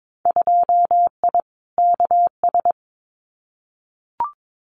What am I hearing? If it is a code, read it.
Morse code, heard as a keyed tone, about 22 wpm, 700 Hz. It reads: 2I KS
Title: Winter Field Day Exchanges Encore 22wpm